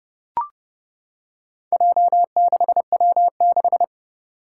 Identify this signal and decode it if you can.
Morse code, heard as a keyed tone, about 29 wpm, 700 Hz. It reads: J6W6
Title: Four Challenging Letter-Number Combinations 30wpm